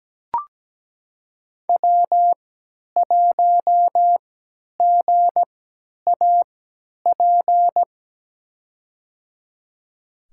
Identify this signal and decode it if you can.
Morse code, heard as a keyed tone, about 17 wpm, spaced as for 10 wpm, 700 Hz. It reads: W1GAP
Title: USA Call Signs - 5 Characters - 3x Character Spacing - 17wpm